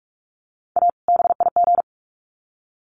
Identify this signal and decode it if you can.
Morse code, heard as a keyed tone, about 44 wpm, 700 Hz. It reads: A BIG
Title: Sets of 2 Words 45wpm